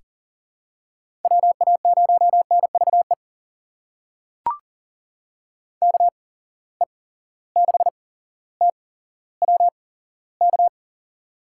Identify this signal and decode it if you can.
Morse code, heard as a keyed tone, about 39 wpm, spaced as for 17 wpm, 700 Hz. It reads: WA0NVE KE6TWK
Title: USA Call Signs - 6 Characters - 8x Character Spacing - 40wpm